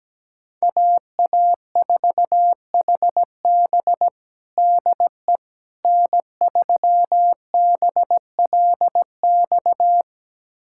Morse code, heard as a keyed tone, about 17 wpm, 700 Hz. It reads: AA4HB DE N3BLX
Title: USA Callsign DE Callsign - 5 Characters - 17wpm